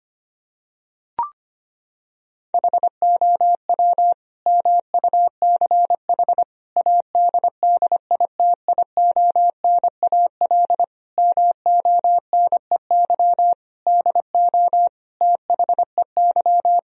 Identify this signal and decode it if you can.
Morse code, heard as a keyed tone, about 25 wpm, 700 Hz. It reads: HOW MUCH ADDITIONAL MONEY DO THEY
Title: Sentences from Top 1000 Words - Review Part 4 - 25wpm